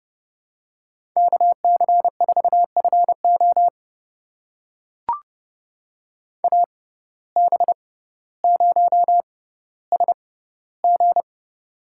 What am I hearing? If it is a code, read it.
Morse code, heard as a keyed tone, about 32 wpm, spaced as for 16 wpm, 700 Hz. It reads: KC4FO AB0SG
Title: USA Call Signs - 5 Characters - 6x Character Spacing - 30wpm